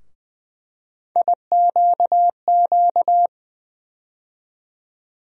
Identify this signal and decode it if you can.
Morse code, heard as a keyed tone, about 21 wpm, 700 Hz. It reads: IQQ
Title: USA Call Sign Postfixes - 3 Letters 20wpm